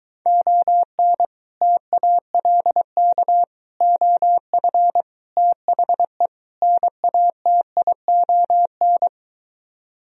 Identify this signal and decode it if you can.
Morse code, heard as a keyed tone, about 23 wpm, 700 Hz. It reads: ON TALK OF THE NATION